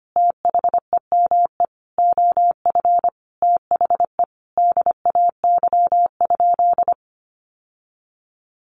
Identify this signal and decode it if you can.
Morse code, heard as a keyed tone, about 25 wpm, 700 Hz. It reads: THEME OF THE DAY?
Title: Sentences from Top 1100 Words 25wpm